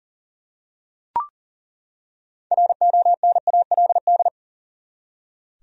Morse code, heard as a keyed tone, about 40 wpm, 700 Hz. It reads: RONALD
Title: USA Common Names - 40wpm